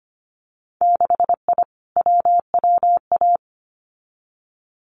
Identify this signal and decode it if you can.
Morse code, heard as a keyed tone, about 25 wpm, 700 Hz. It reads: 6I WWA
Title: Winter Field Day Exchanges Encore 25wpm